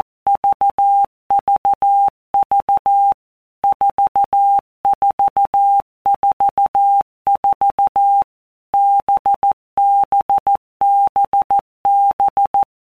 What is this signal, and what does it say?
Morse code, heard as a keyed tone, about 14 wpm, 800 Hz. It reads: VVV 4444 BBBB
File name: Basic Letter Groups By Blocks / Groups@13 / VB46@13